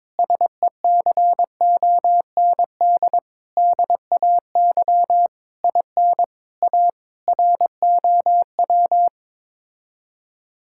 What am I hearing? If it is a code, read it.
Morse code, heard as a keyed tone, about 22 wpm, 700 Hz. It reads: SECOND DAY IN A ROW
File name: practice / 22wpm / Sets of 7 Words - 22wpm